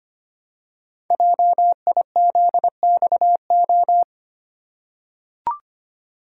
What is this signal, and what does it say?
Morse code, heard as a keyed tone, about 25 wpm, 700 Hz. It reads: JIZXO
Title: Code Groups - 5 Letters - 25wpm